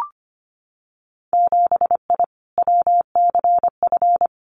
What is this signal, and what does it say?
Morse code, heard as a keyed tone, about 25 wpm, 700 Hz. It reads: 7I WCF
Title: Winter Field Day Exchanges Encore 25wpm